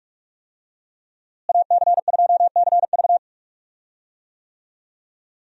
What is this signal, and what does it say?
Morse code, heard as a keyed tone, about 41 wpm, 700 Hz. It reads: AC2CV
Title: USA Call Signs - 5 Characters - Repeated 3x - 45wpm